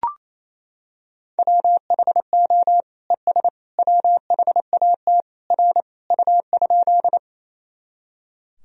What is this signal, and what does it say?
Morse code, heard as a keyed tone, about 28 wpm, 700 Hz. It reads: WHO ES WHAT R U?
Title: Sentences from Top 100 Words and Abbreviations 28wpm